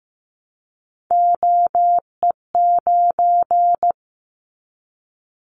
Morse code, heard as a keyed tone, about 15 wpm, 700 Hz. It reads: OE9
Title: Callsign Prefixes for Top 50 DXCC - 15wpm